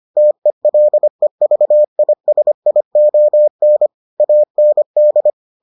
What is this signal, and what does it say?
Morse code, heard as a keyed tone, about 25 wpm, 600 Hz. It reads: TELEVISION AND